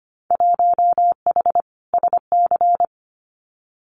Morse code, heard as a keyed tone, about 25 wpm, 700 Hz. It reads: 1H SC
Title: Winter Field Day Exchanges 25wpm